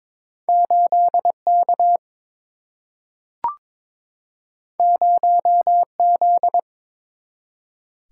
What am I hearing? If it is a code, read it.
Morse code, heard as a keyed tone, about 22 wpm, 700 Hz. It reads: 8 K 0 Z
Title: Single-Letter Single-Number 22wpm